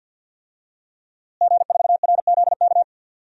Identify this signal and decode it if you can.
Morse code, heard as a keyed tone, about 51 wpm, 700 Hz. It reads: G4RZK
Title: DXCC-100 Call Signs - 5 Characters - Repeated 3x - 50wpm